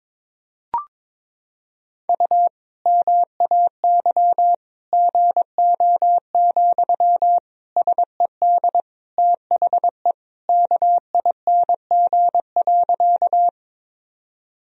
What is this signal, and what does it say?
Morse code, heard as a keyed tone, about 22 wpm, 700 Hz. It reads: U MAY GO, SED THE KING.
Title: Sentences from Top 400 Words and Abbreviations 22wpm